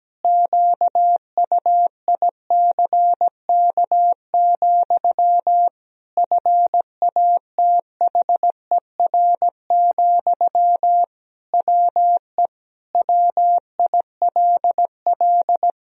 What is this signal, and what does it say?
Morse code, heard as a keyed tone, about 17 wpm, 700 Hz. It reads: QUICK, FATHER, WE WILL
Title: Sentences from Top 1000 Words - Review Part 3 - 17wpm